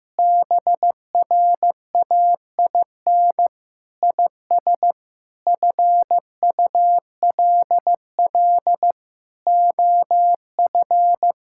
Morse code, heard as a keyed tone, about 15 wpm, 700 Hz. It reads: BRAIN IS FULL OF
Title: Sentences from Top 1000 Words - Review Part 6 - 15wpm